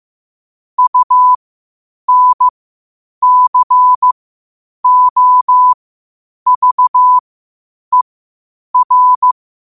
Morse code, heard as a keyed tone, about 15 wpm, spaced as for 8 wpm, 1000 Hz. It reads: UNCOVER
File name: qotdfiles / 8wpm